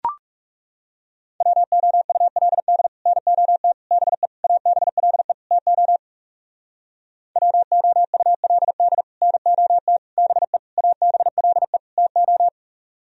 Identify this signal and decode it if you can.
Morse code, heard as a keyed tone, about 36 wpm, 700 Hz. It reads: WOULD NOT BE ABLE TO WOULD NOT BE ABLE TO
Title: Speed-Racing - Sets of 5 Words 30wpm